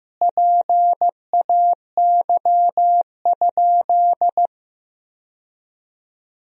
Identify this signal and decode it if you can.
Morse code, heard as a keyed tone, about 15 wpm, 700 Hz. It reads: PAY?
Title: Sentences from Top 1000 Words - Review Part 4 - 15wpm